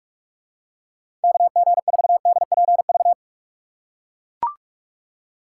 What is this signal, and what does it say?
Morse code, heard as a keyed tone, about 42 wpm, 700 Hz. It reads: KG4DPV
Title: USA Call Signs - 6 Characters - 4x Character Spacing - 45wpm